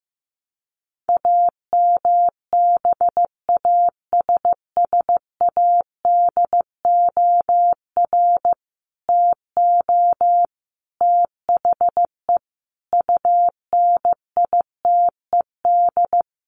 Morse code, heard as a keyed tone, about 15 wpm, 700 Hz. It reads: AMBASSADOR TO THE UNITED
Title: Sets of 5 Words Encore2 15wpm